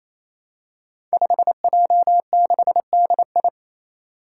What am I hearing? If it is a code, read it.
Morse code, heard as a keyed tone, about 28 wpm, 700 Hz. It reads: 5J6DI
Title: Five Challenging Letter-Number Combinations 28wpm